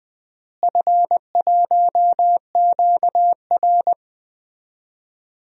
Morse code, heard as a keyed tone, about 21 wpm, 700 Hz. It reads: F1QR
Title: DXCC-100 Call Signs - 4 Characters - Repeated 3x - 20wpm